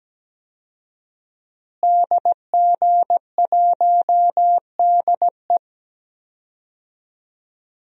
Morse code, heard as a keyed tone, about 17 wpm, 700 Hz. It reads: DG1DE
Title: DXCC-100 Call Signs - 5 Characters - Repeated 3x - 17wpm